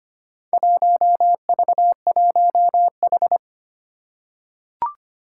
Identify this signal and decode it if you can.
Morse code, heard as a keyed tone, about 25 wpm, 700 Hz. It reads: 1V1H
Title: Four Challenging Letter-Number Combinations 25wpm